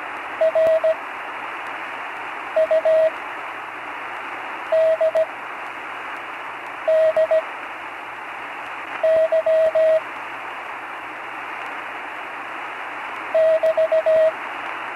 Morse code, heard as a keyed tone, about 17 wpm, spaced as for 5 wpm, 640 Hz, with radio noise. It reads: RUDDY <BT>
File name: CW Learning Programs / Learning Morse Code (from Tune in the World) / 43 on the air